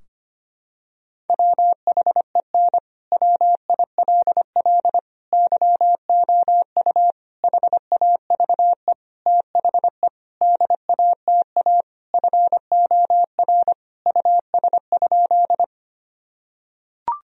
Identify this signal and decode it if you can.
Morse code, heard as a keyed tone, about 25 wpm, 700 Hz. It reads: WHEN WILL YOU HAVE THE DATA FOR US?
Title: Sentences from Top 1000 Words - Review Part 3 - 25wpm